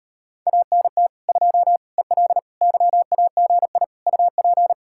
Morse code, heard as a keyed tone, about 33 wpm, 700 Hz. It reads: ANT 2 EL YAGI UP